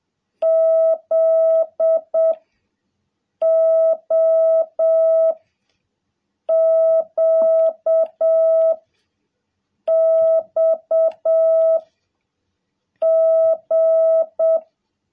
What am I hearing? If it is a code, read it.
Morse code, heard as a keyed tone, about 7 wpm, 660 Hz. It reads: Z O Q X G
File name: jh8cbh / 2021cw